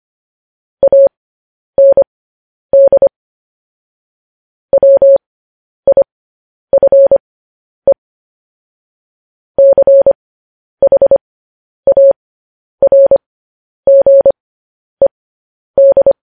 Morse code, heard as a keyed tone, about 25 wpm, spaced as for 10 wpm, 550 Hz. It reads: AND WIFE CHARGED
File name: podcast / News headlines-1x-25x10